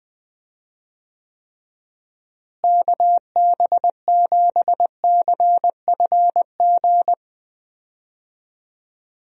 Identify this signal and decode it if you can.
Morse code, heard as a keyed tone, about 20 wpm, 700 Hz. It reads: KB7CFG
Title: USA Call Signs - 6 Characters - Repeated 3x - 20wpm